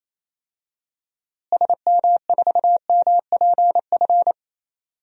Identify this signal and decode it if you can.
Morse code, heard as a keyed tone, about 27 wpm, 700 Hz. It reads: SM4MPF
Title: DXCC-100 Call Signs - 6 Characters - Repeated 2x - 28wpm